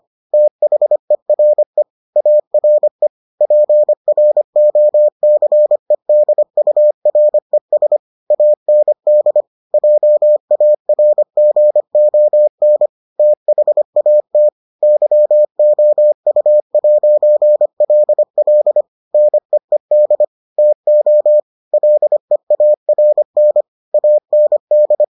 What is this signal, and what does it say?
Morse code, heard as a keyed tone, about 25 wpm, 600 Hz. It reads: THERE ARE PROCEDURES AND JARGON THAT YOU'LL NEED TO LEARN AND